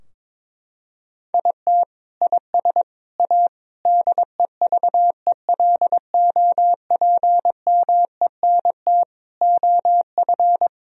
Morse code, heard as a keyed tone, about 22 wpm, 700 Hz. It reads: IT IS A DEVELOPMENT OF
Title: Sentences from Top 1000 Words - Review Part 4 - 22wpm